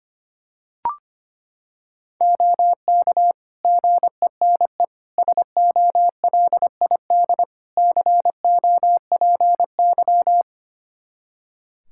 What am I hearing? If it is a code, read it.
Morse code, heard as a keyed tone, about 25 wpm, 700 Hz. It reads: OK GENE SOLID COPY
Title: Practice QSO - Exchange 3 - More Info - 25wpm